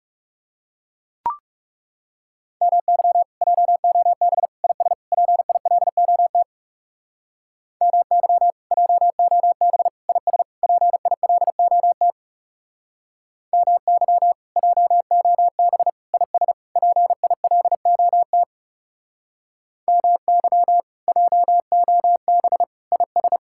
Morse code, heard as a keyed tone, about 36 wpm, 700 Hz. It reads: MY JOB IS PILOT MY JOB IS PILOT MY JOB IS PILOT MY JOB IS
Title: Speed-Racing Common QSO Phrases Encore - 30wpm